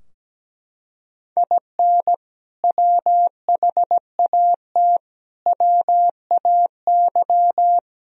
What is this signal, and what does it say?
Morse code, heard as a keyed tone, about 17 wpm, 700 Hz. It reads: IN WHAT WAY
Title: Sentences from Top 1000 Words - Review Part 4 - 17wpm